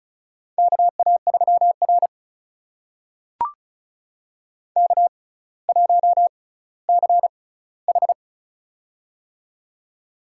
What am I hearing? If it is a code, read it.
Morse code, heard as a keyed tone, about 36 wpm, spaced as for 18 wpm, 700 Hz. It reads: KA3R K1CH
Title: USA Call Signs - 6x Character Spacing - 35wpm